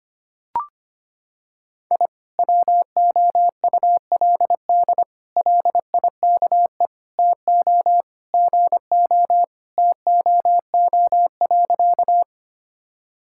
Morse code, heard as a keyed tone, about 25 wpm, 700 Hz. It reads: I WOULD LIKE TO GO TOO.
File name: practice / 25wpm / Sentences with 3-Word Phrases - 25wpm